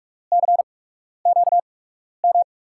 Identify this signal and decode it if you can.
Morse code, heard as a keyed tone, about 44 wpm, spaced as for 12 wpm, 700 Hz. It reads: CQM
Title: USA Call Signs - 8x Character Spacing - 45wpm